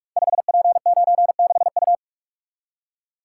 Morse code, heard as a keyed tone, about 44 wpm, 700 Hz. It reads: 5P96U
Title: Code Groups - 5 Letter:Numbers - 45wpm